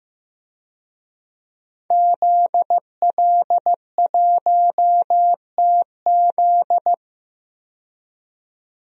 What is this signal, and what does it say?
Morse code, heard as a keyed tone, about 15 wpm, 700 Hz. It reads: ZL1TZ
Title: DXCC-100 Call Signs - 5 Characters - 15wpm